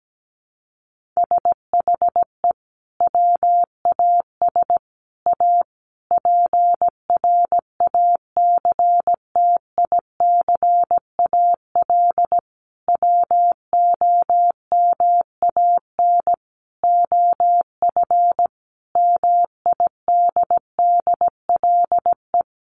Morse code, heard as a keyed tone, about 17 wpm, 700 Hz. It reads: SHE WAS A PRACTICAL WOMAN OF MIDDLE
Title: Sentences from Top 1400 Words 17wpm